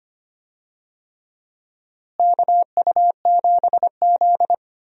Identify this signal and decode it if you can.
Morse code, heard as a keyed tone, about 25 wpm, 700 Hz. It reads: KU7Z
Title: USA Call Signs - 4 Characters - Repeated 3x - 25wpm